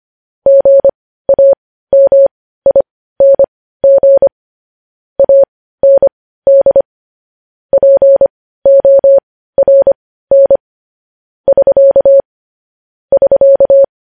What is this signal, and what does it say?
Morse code, heard as a keyed tone, about 25 wpm, spaced as for 15 wpm, 550 Hz. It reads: GAMING AND PORN <SK> <SK>
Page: news-headlines-in-morse-code-at-15-wpm